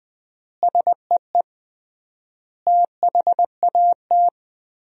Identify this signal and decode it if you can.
Morse code, heard as a keyed tone, about 20 wpm, 700 Hz. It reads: SEE THAT
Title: 3x Word-Spacing - Sentences from Top 500 Words - 20wpm